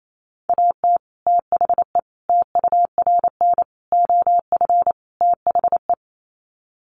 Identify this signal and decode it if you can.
Morse code, heard as a keyed tone, about 28 wpm, 700 Hz. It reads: AT THE TURN OF THE
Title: Sets of 5 Words 28wpm